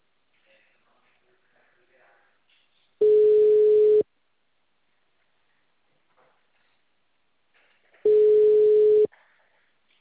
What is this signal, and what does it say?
Ringback tone (Europe, 425 Hz), with speech.
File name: out- -2005- - - .77